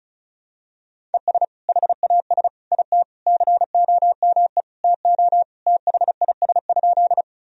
Morse code, heard as a keyed tone, about 36 wpm, 700 Hz. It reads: ES HAS IT COME TO THIS?
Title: Sentences from Top 100 Words and Abbreviations 35wpm